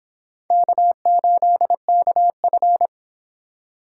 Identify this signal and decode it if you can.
Morse code, heard as a keyed tone, about 26 wpm, 700 Hz. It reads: K8KF